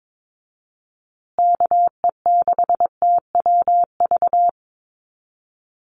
Morse code, heard as a keyed tone, about 22 wpm, 700 Hz. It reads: KE6TWV
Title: USA Call Signs Repeated 2x 22wpm